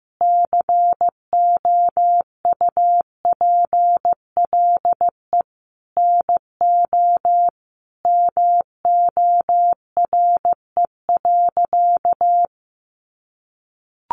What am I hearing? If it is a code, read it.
Morse code, heard as a keyed tone, about 15 wpm, 700 Hz. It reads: COUPLE NO MORE.
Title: Sentences from Top 1100 Words 15wpm